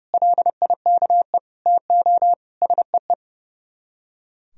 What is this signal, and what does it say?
Morse code, heard as a keyed tone, about 36 wpm, 700 Hz. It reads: LIKE TO SEE
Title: Speed-Racing - Sets of 5 Words 30wpm